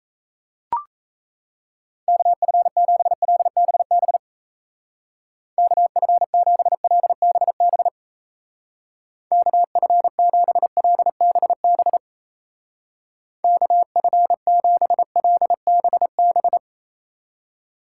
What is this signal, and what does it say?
Morse code, heard as a keyed tone, about 35 wpm, 700 Hz. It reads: KF7LBB KF7LBB KF7LBB KF7LBB
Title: Speed Racing - USA 2x3 Call Signs 28wpm